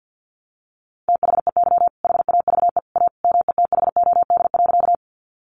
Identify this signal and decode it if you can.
Morse code, heard as a keyed tone, about 50 wpm, 700 Hz. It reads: THEY HAVE A METHOD.
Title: Sentences from Top 600 Words 50wpm